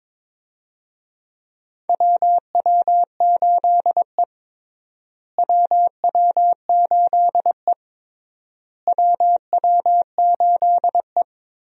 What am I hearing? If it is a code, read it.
Morse code, heard as a keyed tone, about 21 wpm, 700 Hz. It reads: WW8E WW8E WW8E
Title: USA Call Signs - 4 Characters - Repeated 3x - 22wpm